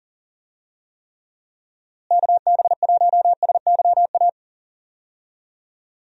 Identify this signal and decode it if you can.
Morse code, heard as a keyed tone, about 37 wpm, 700 Hz. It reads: KB1SYA
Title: USA Call Signs - 6 Characters - Repeated 2x - 40wpm